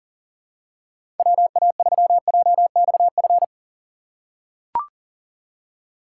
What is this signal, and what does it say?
Morse code, heard as a keyed tone, about 38 wpm, 700 Hz. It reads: WA3JXF
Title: USA Call Signs - 3x Character Spacing - 40wpm